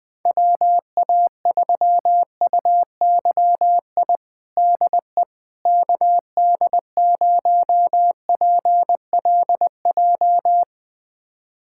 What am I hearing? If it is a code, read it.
Morse code, heard as a keyed tone, about 21 wpm, 700 Hz. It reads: WA3UYI DE KD0PLJ
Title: USA Callsign DE Callsign - 6 Characters - 20wpm